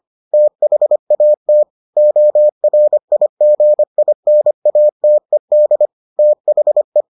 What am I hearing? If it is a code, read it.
Morse code, heard as a keyed tone, about 25 wpm, 600 Hz. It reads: THAT ORIGINATED THE